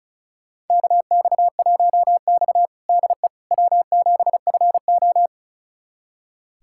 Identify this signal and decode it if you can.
Morse code, heard as a keyed tone, about 36 wpm, 700 Hz. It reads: KX1X DE W7FO
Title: USA Callsign DE Callsign - 4 Characters - 35wpm